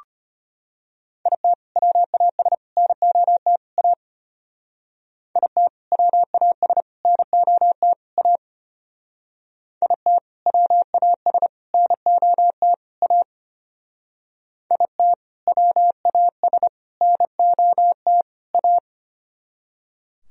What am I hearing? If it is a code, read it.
Morse code, heard as a keyed tone, about 31 wpm, 700 Hz. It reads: IT WAS NOT A IT WAS NOT A IT WAS NOT A IT WAS NOT A
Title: Speed-Racing - Sets of 4 Words 25wpm